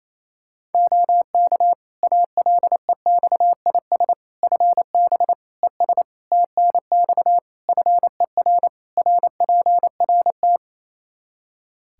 Morse code, heard as a keyed tone, about 28 wpm, 700 Hz. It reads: OK ALEXIS FB ES TNX FER RPRT
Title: Practice QSO - Exchange 2 - Info - 28wpm